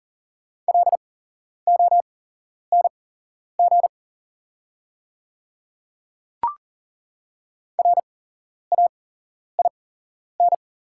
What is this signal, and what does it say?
Morse code, heard as a keyed tone, about 40 wpm, spaced as for 11 wpm, 690 Hz. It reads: LONG RAIN
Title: 8x Character Spacing - Top 500 Words - 40wpm